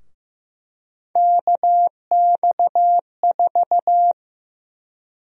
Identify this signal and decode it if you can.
Morse code, heard as a keyed tone, about 15 wpm, 700 Hz. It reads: KX4
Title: USA Call Sign Prefixes - 15wpm